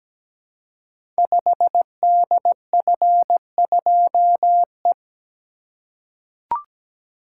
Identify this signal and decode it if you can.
Morse code, heard as a keyed tone, about 17 wpm, 700 Hz. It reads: 5DF2E
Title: Code Groups - 5 Letter:Numbers - 17wpm